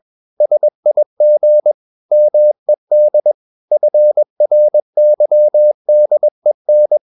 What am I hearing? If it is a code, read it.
Morse code, heard as a keyed tone, about 21 wpm, 600 Hz. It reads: SIG MED FRYDEN
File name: Ferie QRS 21wpm